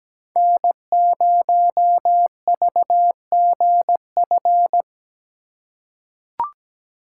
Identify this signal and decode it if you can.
Morse code, heard as a keyed tone, about 17 wpm, 700 Hz. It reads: N0VGF
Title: USA Call Signs - 5 Characters - 2x Character Spacing - 17wpm